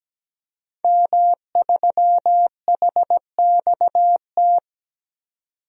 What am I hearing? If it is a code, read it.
Morse code, heard as a keyed tone, about 17 wpm, 700 Hz. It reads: M3HXT
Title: DXCC-100 Call Signs - 5 Characters - 17wpm